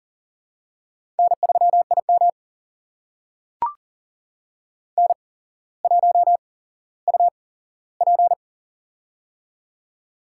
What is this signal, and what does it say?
Morse code, heard as a keyed tone, about 37 wpm, spaced as for 17 wpm, 700 Hz. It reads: N3IM N1UP
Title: USA Call Signs - 4 Characters - 8x Character Spacing - 40wpm